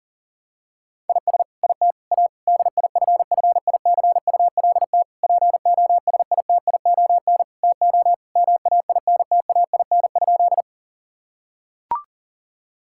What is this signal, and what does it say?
Morse code, heard as a keyed tone, about 40 wpm, 700 Hz. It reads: IS IT A DIFFICULT POSITION TO MAINTAIN?
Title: Sentences from Top 1000 Words - Review Part 5 - 40wpm